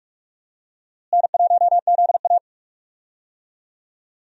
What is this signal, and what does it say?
Morse code, heard as a keyed tone, about 46 wpm, 700 Hz. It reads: N1ZA
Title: USA Call Signs - 4 Characters - Repeated 2x - 45wpm